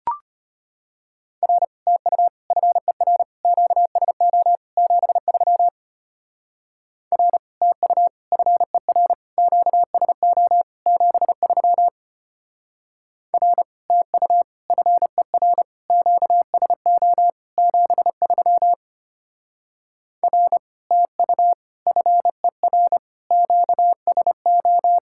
Morse code, heard as a keyed tone, about 30 wpm, 700 Hz. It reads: R TU FER QSO 73 R TU FER QSO 73 R TU FER QSO 73 R TU FER QSO
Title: Speed-Racing Common QSO Phrases - 25wpm